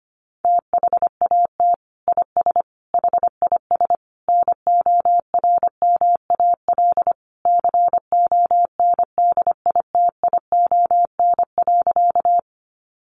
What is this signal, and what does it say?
Morse code, heard as a keyed tone, about 25 wpm, 700 Hz. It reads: THAT IS HIS NORMAL CONDITION.
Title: Sentences from Top 1200 Words 25wpm